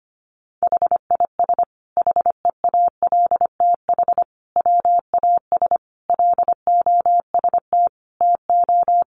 Morse code, heard as a keyed tone, about 25 wpm, 700 Hz. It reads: HIS HEALTH WAS LOST TO
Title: Sentences from Top 1400 Words 25wpm